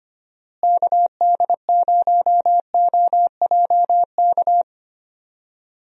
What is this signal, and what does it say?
Morse code, heard as a keyed tone, about 25 wpm, 700 Hz. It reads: KD0OJK
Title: USA Call Signs - 6 Characters - Repeated 3x - 25wpm